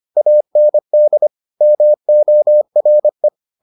Morse code, heard as a keyed tone, about 25 wpm, 600 Hz. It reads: AND MORE